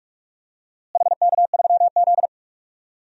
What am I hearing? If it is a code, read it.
Morse code, heard as a keyed tone, about 46 wpm, 690 Hz. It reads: HK3Z
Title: DXCC-100 Call Signs - 4 Characters - Repeated 3x - 45wpm